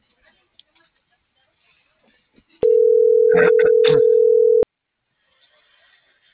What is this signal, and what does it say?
Ringback tone (North America), 440 + 480 Hz.